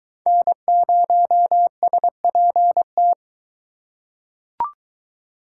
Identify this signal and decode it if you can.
Morse code, heard as a keyed tone, about 22 wpm, 700 Hz. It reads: N0SPT